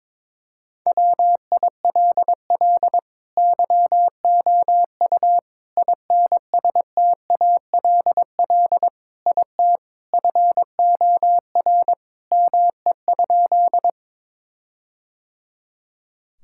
Morse code, heard as a keyed tone, about 22 wpm, 700 Hz. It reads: WILL YOU INSTALL IT FOR ME?
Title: Sentences from Top 1600 Words 22wpm